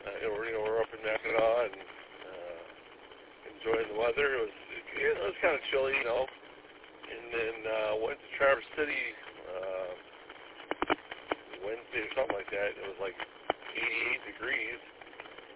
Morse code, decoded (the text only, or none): none